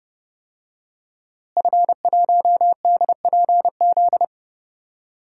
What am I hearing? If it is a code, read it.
Morse code, heard as a keyed tone, about 31 wpm, 700 Hz. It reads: F1DPZ
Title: DXCC-100 Call Signs - 5 Characters - Repeated 3x - 30wpm